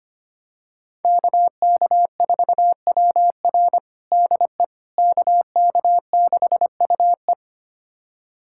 Morse code, heard as a keyed tone, about 25 wpm, 700 Hz. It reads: KK4WR DE KK6UE
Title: USA Callsign DE Callsign - 5 Characters - 25wpm